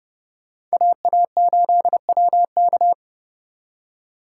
Morse code, heard as a keyed tone, about 38 wpm, 700 Hz. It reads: AA8WK
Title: Speed Racing - USA 2x2 Call Signs 30wpm